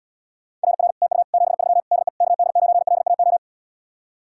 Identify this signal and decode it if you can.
Morse code, heard as a keyed tone, about 79 wpm, 700 Hz. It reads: RR TU 73 DE KD9LAO